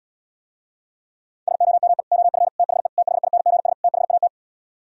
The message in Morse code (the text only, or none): SOME OF THE THINGS THAT